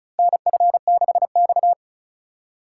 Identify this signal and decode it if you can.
Morse code, heard as a keyed tone, about 36 wpm, 700 Hz. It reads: NF6X
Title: USA Call Signs - 4 Characters - 3x Character Spacing - 35wpm